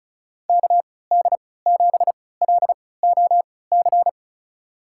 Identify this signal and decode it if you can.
Morse code, heard as a keyed tone, about 36 wpm, spaced as for 20 wpm, 700 Hz. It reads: KD7LOC
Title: USA Call Signs - 6 Characters - 3x Character Spacing - 35wpm